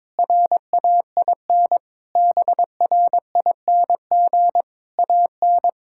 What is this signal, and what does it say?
Morse code, heard as a keyed tone, about 22 wpm, 700 Hz. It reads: RAIN BRING AN
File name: practice / 22wpm / Sentences with 3-Word Phrases - 22wpm